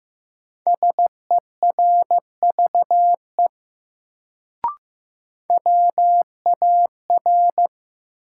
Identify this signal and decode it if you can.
Morse code, heard as a keyed tone, about 15 wpm, 700 Hz. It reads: SERVE WAR
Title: Sentences from Top 400 Words - New Word Drill - 15wpm